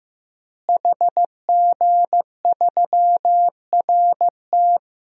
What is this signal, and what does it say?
Morse code, heard as a keyed tone, about 15 wpm, 700 Hz. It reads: HG3RT
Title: DXCC-100 Call Signs - 5 Characters - Repeated 3x - 15wpm